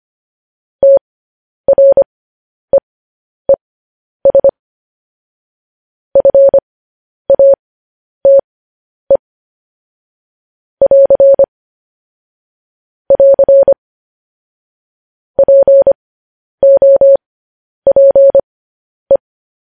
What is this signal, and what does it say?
Morse code, heard as a keyed tone, about 25 wpm, spaced as for 10 wpm, 550 Hz. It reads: TREES FATE <AR> <AR> POPE
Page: news-headlines-in-morse-code-at-10-wpm